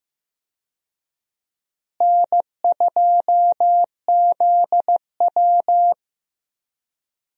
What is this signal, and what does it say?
Morse code, heard as a keyed tone, about 15 wpm, 700 Hz. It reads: N2ZW
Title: USA Call Signs - 4 Characters - Repeated 2x - 15wpm